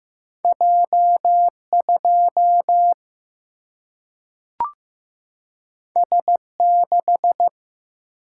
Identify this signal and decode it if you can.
Morse code, heard as a keyed tone, about 15 wpm, 700 Hz. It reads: J 2 S 6
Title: Single-Letter Single-Number 15wpm